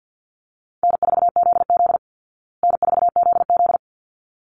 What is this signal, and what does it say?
Morse code, heard as a keyed tone, about 48 wpm, 700 Hz. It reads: N4ZZ N4ZZ
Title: Easy Call Signs Repeated 2x - 50wpm